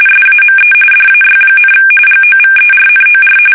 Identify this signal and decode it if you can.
RTTY, about 45.45 baud. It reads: SAME DAY, GOVERNOR
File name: fsk850 / usnavy1